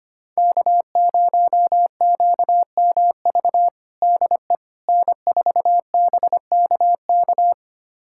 Morse code, heard as a keyed tone, about 25 wpm, 700 Hz. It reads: K0QMV DE N4BKK
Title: USA Callsign DE Callsign - 5 Characters - 25wpm